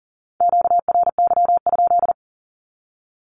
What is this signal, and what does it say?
Morse code, heard as a keyed tone, about 40 wpm, 700 Hz. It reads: QRY?
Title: Q-Codes 40wpm